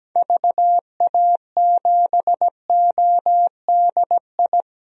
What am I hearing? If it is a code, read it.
Morse code, heard as a keyed tone, about 17 wpm, 700 Hz. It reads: VA7ODI
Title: DXCC-100 Call Signs - 6 Characters - Repeated 2x - 17wpm